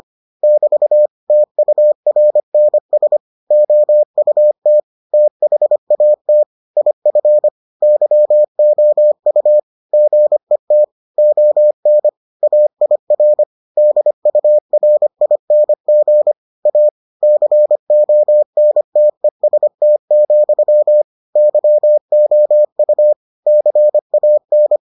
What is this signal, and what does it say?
Morse code, heard as a keyed tone, about 25 wpm, 600 Hz. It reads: <BT> TURNS OUT THAT IF YOU GET ON AIR DURING A CONTEST, YOU CAN